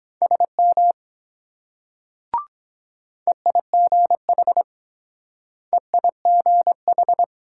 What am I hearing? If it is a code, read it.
Morse code, heard as a keyed tone, about 23 wpm, 700 Hz. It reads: SM EIGH EIGH